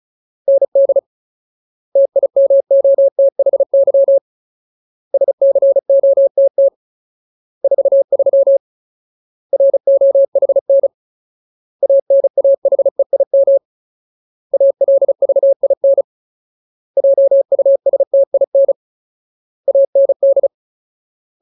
Morse code, heard as a keyed tone, about 34 wpm, 550 Hz. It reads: ND TIMOTHY SCOTT 43 ROHN ANAHEIM ALVIN JUSTIN AND
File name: Ultimate Word List / 35WPM Ultimate